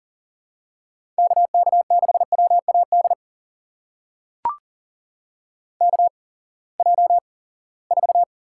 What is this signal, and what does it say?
Morse code, heard as a keyed tone, about 40 wpm, spaced as for 23 wpm, 700 Hz. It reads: KK6WAD KJ4
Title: USA Call Signs - 6 Characters - 8x Character Spacing - 40wpm